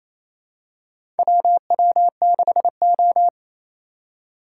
Morse code, heard as a keyed tone, about 28 wpm, 700 Hz. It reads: WW6O
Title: USA Call Signs - 4 Characters - 3x Character Spacing - 28wpm